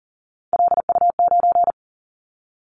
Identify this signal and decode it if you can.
Morse code, heard as a keyed tone, about 42 wpm, 700 Hz. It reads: LU9
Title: Callsign Prefixes for Top 100 DXCC - 40wpm